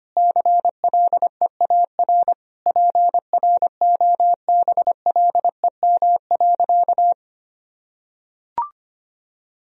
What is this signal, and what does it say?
Morse code, heard as a keyed tone, about 25 wpm, 700 Hz. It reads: CLEAR PROBLEM.
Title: Sentences from Top 1000 Words - Review Part 2 - 25wpm